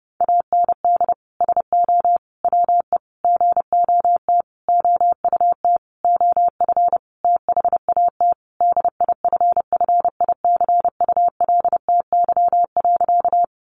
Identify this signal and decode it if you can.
Morse code, heard as a keyed tone, about 30 wpm, 700 Hz. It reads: AND SO WE GOT OUT OF THAT DIFFICULTY.
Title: Sentences from Top 1500 Words 30wpm